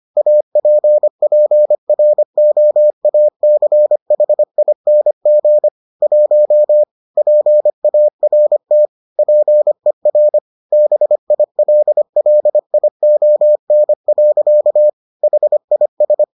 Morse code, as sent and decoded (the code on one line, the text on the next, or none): .- .--. .--. .-. --- .- -.-. .... .. -. --. / .---- / .--. .- .-. - / .--. .-. / -... .. .-.. .-.. .. --- -. .-.-.- / .... .. ...
APPROACHING 1 PART PER BILLION. HIS